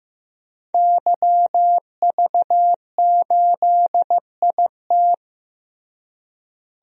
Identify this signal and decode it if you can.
Morse code, heard as a keyed tone, about 15 wpm, 700 Hz. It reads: YV8IT
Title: DXCC-100 Call Signs - 5 Characters - Repeated 3x - 15wpm